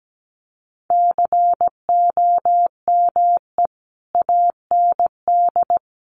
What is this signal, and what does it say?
Morse code, heard as a keyed tone, about 17 wpm, 700 Hz. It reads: COME AND
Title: Sentences from Top 1400 Words 17wpm